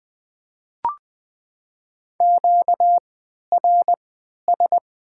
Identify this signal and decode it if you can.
Morse code, heard as a keyed tone, about 20 wpm, spaced as for 11 wpm, 700 Hz. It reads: QRS
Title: Top 90 QSO Elements - 3x Character Spacing - 20wpm